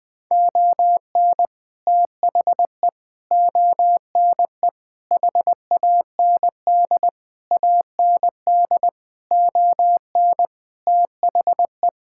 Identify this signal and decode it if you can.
Morse code, heard as a keyed tone, about 20 wpm, 700 Hz. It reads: ON THE ONE HAND AND ON THE
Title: Sets of 8 Words - 20wpm